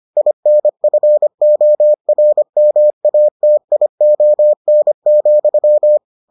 Morse code, heard as a keyed tone, about 25 wpm, 600 Hz. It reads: INFORMATION,